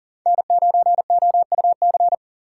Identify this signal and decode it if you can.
Morse code, heard as a keyed tone, about 40 wpm, 700 Hz. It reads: N9OUC
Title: USA Call Signs - 5 Characters - 4x Character Spacing - 40wpm